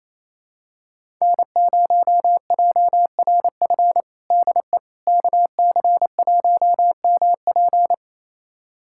Morse code, heard as a keyed tone, about 28 wpm, 700 Hz. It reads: N0JRF DE KC1MP
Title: USA Callsign DE Callsign - 5 Characters - 28wpm